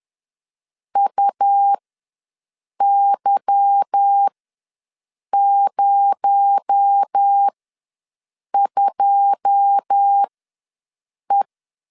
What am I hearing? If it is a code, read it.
Morse code, heard as a keyed tone, about 10 wpm, spaced as for 6 wpm, 800 Hz. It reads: UY02E
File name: LEZIONI / 9 28 ws 10wpm 4124